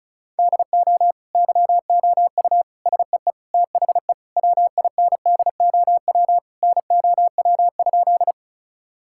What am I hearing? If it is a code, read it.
Morse code, heard as a keyed tone, about 36 wpm, 700 Hz. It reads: DO YOU SEE THE WINDOW NOW?
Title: Sentences from Top 1600 Words 35wpm